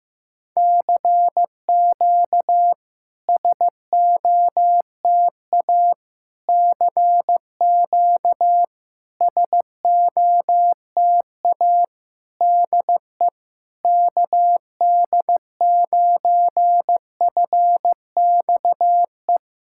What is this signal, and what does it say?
Morse code, heard as a keyed tone, about 15 wpm, 700 Hz. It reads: CQ SOTA CQ SOTA DE KD9FXE
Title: SOTA QSO Practice 15wpm